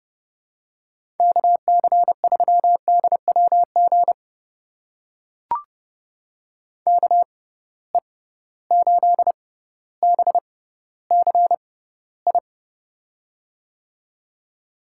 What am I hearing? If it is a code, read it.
Morse code, heard as a keyed tone, about 30 wpm, spaced as for 15 wpm, 700 Hz. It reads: KC3DWG KE8BCI
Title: USA Call Signs - 6 Characters - 6x Character Spacing - 30wpm